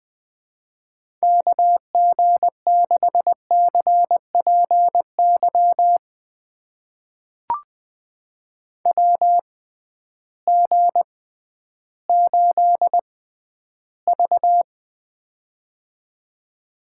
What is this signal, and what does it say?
Morse code, heard as a keyed tone, about 20 wpm, spaced as for 12 wpm, 700 Hz. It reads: KG6CPY WG8V
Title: USA Call Signs - 6x Character Spacing - 20wpm